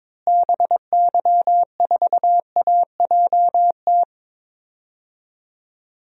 Morse code, heard as a keyed tone, about 22 wpm, 700 Hz. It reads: BY4AJT
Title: DXCC-100 Call Signs - 6 Characters - Repeated 2x - 22wpm